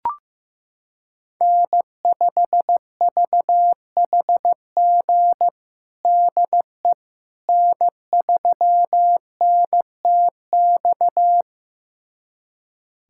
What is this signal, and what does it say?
Morse code, heard as a keyed tone, about 15 wpm, 700 Hz. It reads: N5VHG DE N3NTX
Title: USA Callsign DE Callsign - 5 Characters - 15wpm